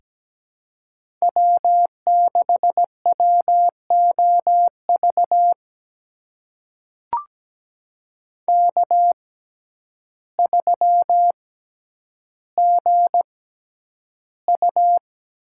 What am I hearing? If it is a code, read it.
Morse code, heard as a keyed tone, about 17 wpm, spaced as for 9 wpm, 700 Hz. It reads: W6WOV K3GU
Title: USA Call Signs - 5 Characters - 6x Character Spacing - 17wpm